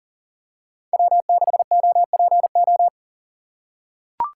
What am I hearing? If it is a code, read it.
Morse code, heard as a keyed tone, about 40 wpm, 700 Hz. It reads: W6OPO
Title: K1USN SST QSO Practice - 40wpm